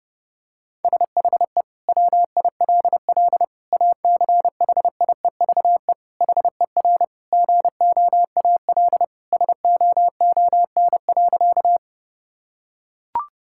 Morse code, heard as a keyed tone, about 32 wpm, 700 Hz. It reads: SHE WILL ACHIEVE HER GOAL SOON.
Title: Sentences from Top 1000 Words - Review Part 6 - 30wpm